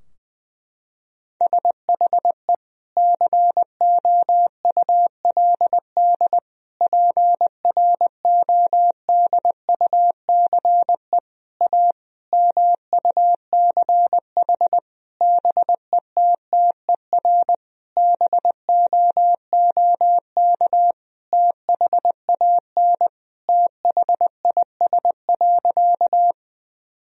Morse code, heard as a keyed tone, about 20 wpm, 700 Hz. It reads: SHE COULD PRODUCE A MUCH BETTER BOOK THAN THIS.
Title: Sentences from Top 1000 Words - Review Part 3 - 20wpm